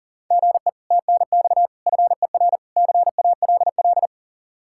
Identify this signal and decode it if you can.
Morse code, heard as a keyed tone, about 53 wpm, 700 Hz. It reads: GE TNX FER CALL